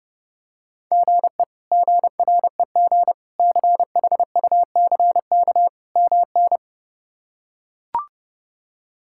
Morse code, heard as a keyed tone, about 30 wpm, 700 Hz. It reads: GE GREG CHUCK MN
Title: K1USN SST QSO Practice - 30wpm